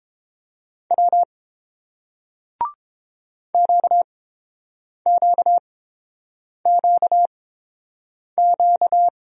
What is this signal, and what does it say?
Morse code, heard as a keyed tone, about 29 wpm, spaced as for 7 wpm, 700 Hz. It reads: W QQQQ